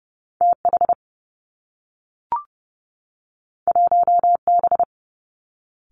Morse code, heard as a keyed tone, about 30 wpm, 700 Hz. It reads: T H 1 B
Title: Two Challenging Letter-Number Combinations 30wpm